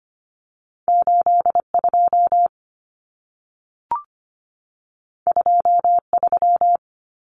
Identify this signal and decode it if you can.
Morse code, heard as a keyed tone, about 25 wpm, 700 Hz. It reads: 82 23
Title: Numbers - 2-Digit 25wpm